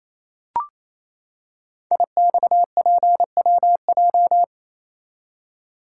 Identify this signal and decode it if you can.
Morse code, heard as a keyed tone, about 28 wpm, 690 Hz. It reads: IXPWJ
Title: Code Groups - 5 Letters - 28wpm